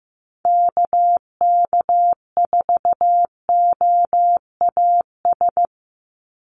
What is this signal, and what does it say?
Morse code, heard as a keyed tone, about 15 wpm, 700 Hz. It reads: KK4OAS
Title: USA Call Signs 15wpm